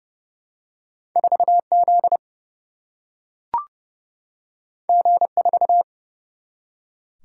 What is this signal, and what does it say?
Morse code, heard as a keyed tone, about 30 wpm, 700 Hz. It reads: 4 Z G 4
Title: Single-Letter Single-Number 30wpm